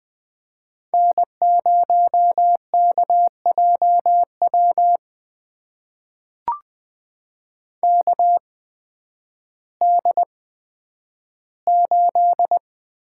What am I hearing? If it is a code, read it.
Morse code, heard as a keyed tone, about 20 wpm, spaced as for 11 wpm, 700 Hz. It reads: N0KJW KD8
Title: USA Call Signs - 8x Character Spacing - 20wpm